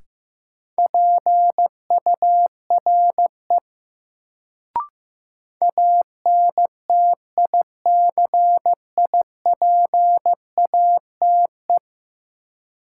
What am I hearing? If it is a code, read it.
Morse code, heard as a keyed tone, about 15 wpm, 700 Hz. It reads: PURE ANTICIPATE
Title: Sentences from Top 1500 Words - New Word Drill Alt - 15wpm